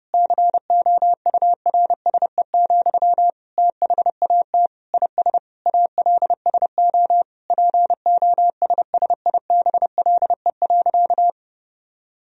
Morse code, heard as a keyed tone, about 32 wpm, 700 Hz. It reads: COURSE, THAT IS ALSO POSSIBLE.
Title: Sentences from Top 1000 Words - Review Part 1 - 30wpm